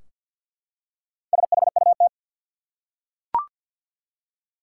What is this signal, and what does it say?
Morse code, heard as a keyed tone, about 51 wpm, 700 Hz. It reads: SHUT
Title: Four Letter Words - 50wpm